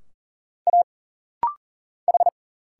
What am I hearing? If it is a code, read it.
Morse code, heard as a keyed tone, about 40 wpm, 700 Hz. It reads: A H
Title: Single Letters - Rapid-Fire 40wpm